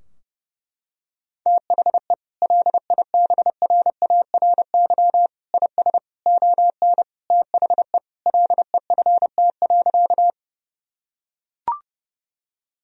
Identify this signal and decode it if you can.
Morse code, heard as a keyed tone, about 32 wpm, 700 Hz. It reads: THE LIBRARY IS ON THE LEFT.
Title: Sentences from Top 1000 Words - Review Part 4 - 30wpm